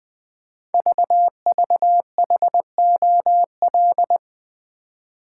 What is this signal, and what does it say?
Morse code, heard as a keyed tone, about 20 wpm, 700 Hz. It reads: VVHOL
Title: Code Groups - 5 Letters - 20wpm